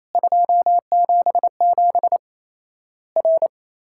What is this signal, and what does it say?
Morse code, heard as a keyed tone, about 28 wpm, 700 Hz. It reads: 277 R
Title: Worked All Europe DX Contest - QTC Exchange - 28wpm